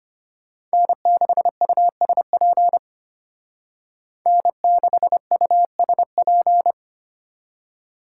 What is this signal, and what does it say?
Morse code, heard as a keyed tone, about 33 wpm, 700 Hz. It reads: N6USP N6USP